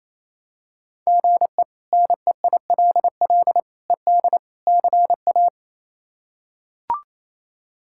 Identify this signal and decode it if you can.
Morse code, heard as a keyed tone, about 28 wpm, 700 Hz. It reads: GE NEILL ED CA
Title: K1USN SST QSO Practice - 28wpm